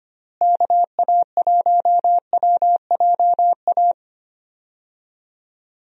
Morse code, heard as a keyed tone, about 25 wpm, 700 Hz. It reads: KA1WJA
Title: USA Call Signs - 6 Characters - Repeated 3x - 25wpm